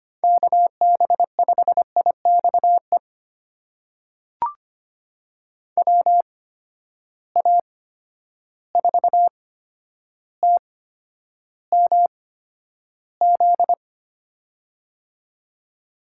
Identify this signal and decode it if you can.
Morse code, heard as a keyed tone, about 25 wpm, spaced as for 11 wpm, 700 Hz. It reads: KB5IXE WA4TMZ
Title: USA Call Signs - 6 Characters - 8x Character Spacing - 25wpm